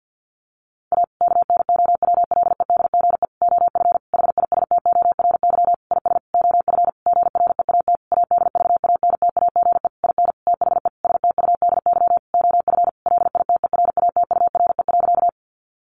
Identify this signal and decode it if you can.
Morse code, heard as a keyed tone, about 48 wpm, 700 Hz. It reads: A KNOWLEDGE OF HISTORY IS OF GREAT ADVANTAGE IN THE STUDY OF LITERATURE.
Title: Sentences from Top 900 Words 50wpm